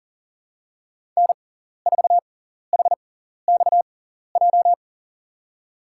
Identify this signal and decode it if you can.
Morse code, heard as a keyed tone, about 40 wpm, spaced as for 14 wpm, 700 Hz. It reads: N4HXJ
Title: USA Call Signs - 5 Characters - 6x Character Spacing - 40wpm